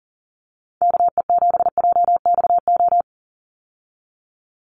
Morse code, heard as a keyed tone, about 40 wpm, 700 Hz. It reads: KE7JXO
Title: USA Call Signs 40wpm